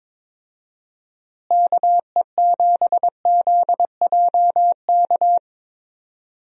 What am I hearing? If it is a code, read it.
Morse code, heard as a keyed tone, about 22 wpm, 700 Hz. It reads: KE7ZJK
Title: USA Call Signs - 6 Characters - Repeated 2x - 22wpm